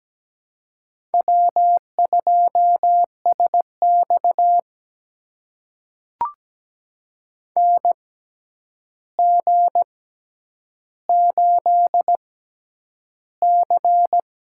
Morse code, heard as a keyed tone, about 17 wpm, spaced as for 9 wpm, 700 Hz. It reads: W2SX NG8C
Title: USA Call Signs - 4 Characters - 6x Character Spacing - 17wpm